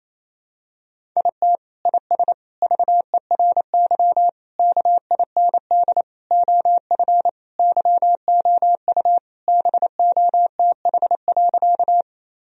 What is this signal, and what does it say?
Morse code, heard as a keyed tone, about 28 wpm, 700 Hz. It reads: IT IS VERY KIND OF YOU BOTH.
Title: Sentences from Top 1000 Words - Review Part 1 - 28wpm